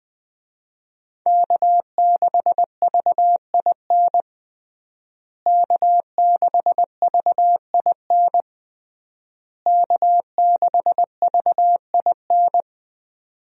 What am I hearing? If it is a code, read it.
Morse code, heard as a keyed tone, about 20 wpm, 700 Hz. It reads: K6VIN K6VIN K6VIN
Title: USA Call Signs - 5 Characters - Repeated 3x - 20wpm